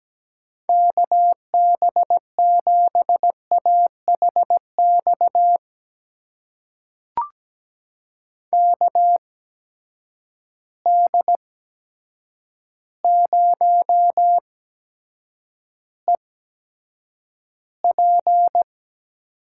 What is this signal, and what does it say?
Morse code, heard as a keyed tone, about 17 wpm, spaced as for 8 wpm, 700 Hz. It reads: KB7AHX KD0EP
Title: USA Call Signs - 6 Characters - 8x Character Spacing - 17wpm